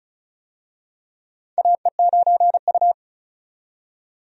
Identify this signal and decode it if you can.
Morse code, heard as a keyed tone, about 36 wpm, 700 Hz. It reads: AE9U
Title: USA Call Signs - 4 Characters - 35wpm